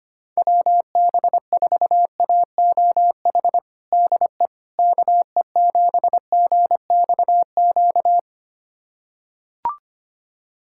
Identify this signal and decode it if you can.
Morse code, heard as a keyed tone, about 25 wpm, 700 Hz. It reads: WB4AOH DE KE7GXQ
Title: USA Callsign DE Callsign 25wpm